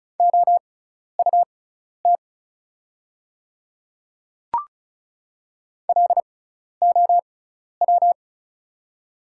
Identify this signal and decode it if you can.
Morse code, heard as a keyed tone, about 36 wpm, spaced as for 12 wpm, 700 Hz. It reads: OUT LOW
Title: 6x Character Spacing - Top 200 Words - 35wpm